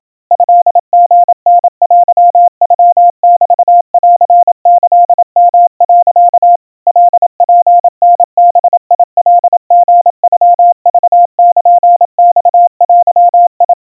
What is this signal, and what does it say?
Morse code (Japanese Wabun code), heard as a keyed tone, about 28 wpm, 700 Hz. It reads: トリタテノメンキヨ、 カツタバカリノクルマデ
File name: kawa / Morse sample / kamyu 100 cpm 700 Hz Crispy